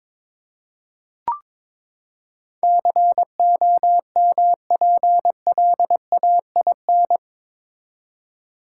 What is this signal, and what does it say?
Morse code, heard as a keyed tone, about 21 wpm, 700 Hz. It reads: COMPLAIN
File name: practice / 22wpm / Eight Letter Words - 22wpm